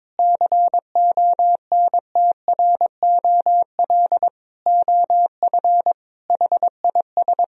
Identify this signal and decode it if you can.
Morse code, heard as a keyed tone, about 21 wpm, 700 Hz. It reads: CONTROL OF HIS
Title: Sentences from Top 1000 Words - Review Part 3 - 22wpm